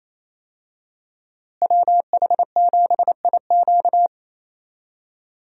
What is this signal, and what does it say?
Morse code, heard as a keyed tone, about 28 wpm, 700 Hz. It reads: WH7IQ
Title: USA Call Signs - 5 Characters - Repeated 3x - 28wpm